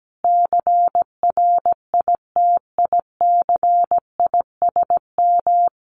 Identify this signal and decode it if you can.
Morse code, heard as a keyed tone, about 17 wpm, 700 Hz. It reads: CRITICISM
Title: Sentences from Top 1300 Words 17wpm